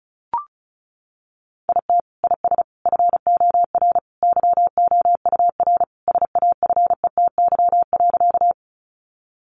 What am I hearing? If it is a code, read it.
Morse code, heard as a keyed tone, about 34 wpm, 700 Hz. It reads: IT IS FOR YOUR SAFETY.
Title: Sentences from Top 700 Words 35wpm